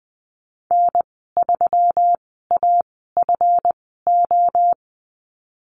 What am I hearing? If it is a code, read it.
Morse code, heard as a keyed tone, about 20 wpm, spaced as for 14 wpm, 700 Hz. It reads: N3AFO
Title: 161 Lesson 26g - Letter g - Call Signs 20wpm